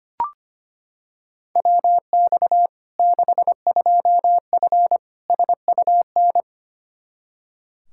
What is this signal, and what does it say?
Morse code, heard as a keyed tone, about 25 wpm, 700 Hz. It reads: WX 62F SUN
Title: Practice QSO - Exchange 2 - Info - 25wpm